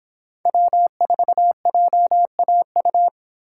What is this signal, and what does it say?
Morse code, heard as a keyed tone, about 26 wpm, 700 Hz. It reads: W4JAU
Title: Speed Racing - USA Call Signs 22wpm